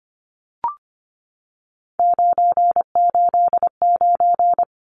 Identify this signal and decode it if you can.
Morse code, heard as a keyed tone, about 25 wpm, 700 Hz. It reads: 989
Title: Numbers - 3-Digit 25wpm